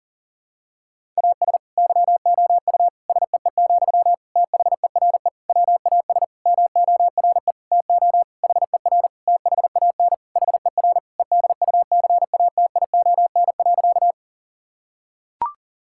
Morse code, heard as a keyed tone, about 40 wpm, 700 Hz. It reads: AS YOU SEE, THERE WAS MORE TO HER THAN HER EDUCATION.
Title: Sentences from Top 1000 Words - Review Part 6 - 40wpm